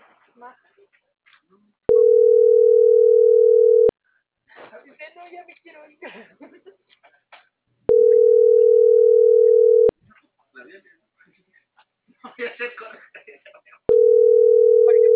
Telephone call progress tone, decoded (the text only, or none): ringback tone